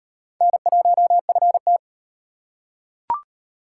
Morse code, heard as a keyed tone, about 36 wpm, 700 Hz. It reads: N1FT